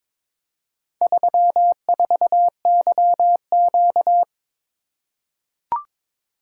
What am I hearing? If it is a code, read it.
Morse code, heard as a keyed tone, about 22 wpm, 700 Hz. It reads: 34YQ
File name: practice / 22wpm / Four Letter-Number Combinations - 22wpm